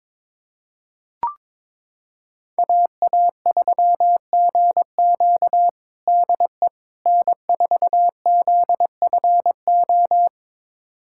Morse code, heard as a keyed tone, about 22 wpm, 700 Hz. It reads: AA3GQ DE N4ZFO
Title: USA Callsign DE Callsign - 5 Characters - 22wpm